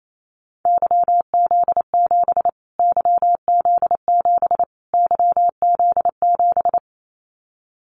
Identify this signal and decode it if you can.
Morse code, heard as a keyed tone, about 28 wpm, 700 Hz. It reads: YZ7 YZ7 YZ7
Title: Callsign Prefixes for Top 25 DXCC - Repeated 3x - 28wpm